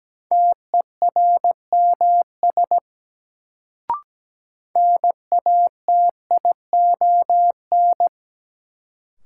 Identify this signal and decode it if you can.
Morse code, heard as a keyed tone, about 17 wpm, 700 Hz. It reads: TERMS NATION
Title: Sets of 3 Words - New Word Drill - 17wpm